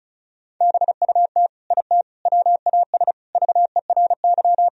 Morse code, heard as a keyed tone, about 36 wpm, 700 Hz. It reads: BUT IT WAS VERY
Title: Sentences from Top 1000 Words - Review Part 2 - 35wpm